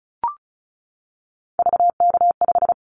Morse code, heard as a keyed tone, about 36 wpm, 700 Hz. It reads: VK5
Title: Callsign Prefixes for Top 50 DXCC - 35wpm